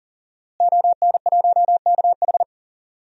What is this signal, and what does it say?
Morse code, heard as a keyed tone, about 39 wpm, 700 Hz. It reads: ON1KH
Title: DXCC-100 Call Signs - 5 Characters - Repeated 3x - 40wpm